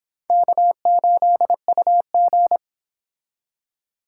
Morse code, heard as a keyed tone, about 26 wpm, 700 Hz. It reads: K8UG